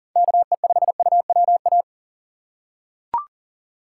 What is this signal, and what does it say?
Morse code, heard as a keyed tone, about 40 wpm, 700 Hz. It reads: KE5UWA
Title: USA Call Signs - 6 Characters - 3x Character Spacing - 40wpm